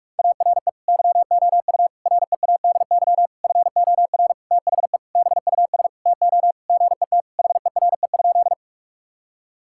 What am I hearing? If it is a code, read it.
Morse code, heard as a keyed tone, about 46 wpm, 700 Hz. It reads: ARE YOU READY FOR THE BUS TO GET HERE?
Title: Sentences from Top 1000 Words - Review Part 6 - 45wpm